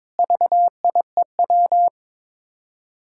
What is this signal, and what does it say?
Morse code, heard as a keyed tone, about 22 wpm, 700 Hz. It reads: VIEW
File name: practice / 22wpm / Build A Phrase - 5-Word Phrases - 22wpm